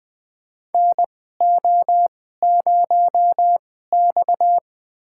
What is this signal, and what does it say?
Morse code, heard as a keyed tone, about 20 wpm, spaced as for 15 wpm, 700 Hz. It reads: NO0X
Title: USA Call Signs - 4 Characters - 2x Character Spacing - 20wpm